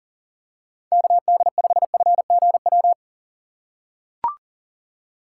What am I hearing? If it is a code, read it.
Morse code, heard as a keyed tone, about 40 wpm, 700 Hz. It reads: KD5FGW
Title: USA Call Signs - 6 Characters - 2x Character Spacing - 40wpm